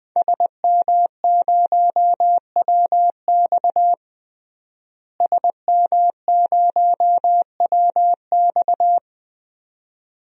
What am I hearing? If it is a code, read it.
Morse code, heard as a keyed tone, about 21 wpm, 700 Hz. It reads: SM0WX SM0WX
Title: DXCC-100 Call Signs - 5 Characters - Repeated 2x - 20wpm